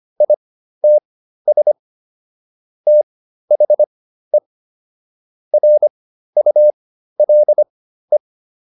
Morse code, heard as a keyed tone, about 25 wpm, spaced as for 13 wpm, 600 Hz. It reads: ITS THE RULE